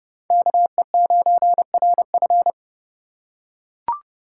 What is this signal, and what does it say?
Morse code, heard as a keyed tone, about 30 wpm, 700 Hz. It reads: KE9RF
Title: USA Call Signs - 5 Characters - 4x Character Spacing - 30wpm